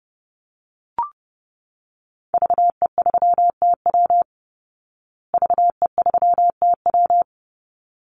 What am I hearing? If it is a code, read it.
Morse code, heard as a keyed tone, about 28 wpm, 700 Hz. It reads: VE3TW VE3TW
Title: Call Signs Repeated 3x - 30wpm